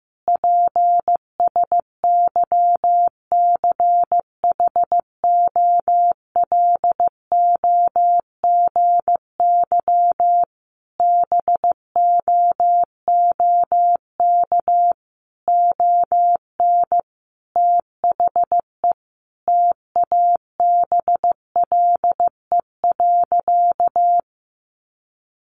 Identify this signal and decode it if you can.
Morse code, heard as a keyed tone, about 14 wpm, 700 Hz. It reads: PSYCHOLOGY BOOK ON THE TABLE.
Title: Sentences from Top 1400 Words 15wpm